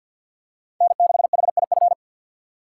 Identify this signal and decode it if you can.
Morse code, heard as a keyed tone, about 52 wpm, 700 Hz. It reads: N6HIF
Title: USA Call Signs - 5 Characters - 6x Character Spacing - 50wpm